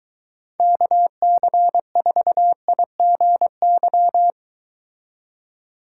Morse code, heard as a keyed tone, about 22 wpm, 700 Hz. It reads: KC4IGY